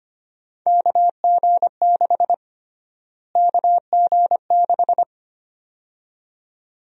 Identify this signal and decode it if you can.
Morse code, heard as a keyed tone, about 25 wpm, 700 Hz. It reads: KG6 KG6
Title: USA Call Sign Prefixes - Repeated 2x - 25wpm